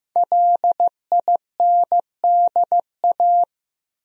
Morse code, heard as a keyed tone, about 15 wpm, 700 Hz. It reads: LINDA
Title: USA Common Names - Max 5 Letters - 15wpm